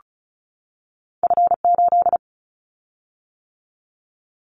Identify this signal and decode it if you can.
Morse code, heard as a keyed tone, about 34 wpm, 700 Hz. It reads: F8
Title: Callsign Prefixes for Top 50 DXCC - 35wpm